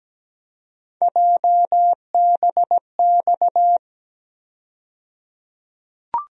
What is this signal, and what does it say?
Morse code, heard as a keyed tone, about 17 wpm, 700 Hz. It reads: JBX
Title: USA Call Sign Postfixes - 3 Letters 17wpm